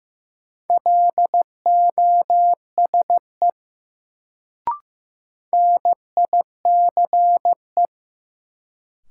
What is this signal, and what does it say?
Morse code, heard as a keyed tone, about 15 wpm, 700 Hz. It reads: LOSE NICE
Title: Sentences from Top 1100 Words - New Word Drill - 15wpm